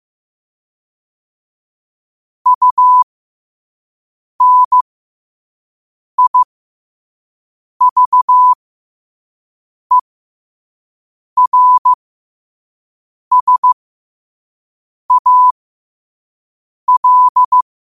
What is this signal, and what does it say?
Morse code, heard as a keyed tone, about 15 wpm, spaced as for 5 wpm, 1000 Hz. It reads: UNIVERSAL
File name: qotdfiles / 5wpm